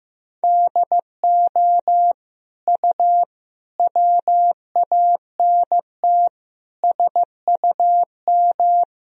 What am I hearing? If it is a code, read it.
Morse code, heard as a keyed tone, about 15 wpm, 700 Hz. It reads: DO U WANT SUM
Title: Sentences from Top 300 Words and Abbreviations 15wpm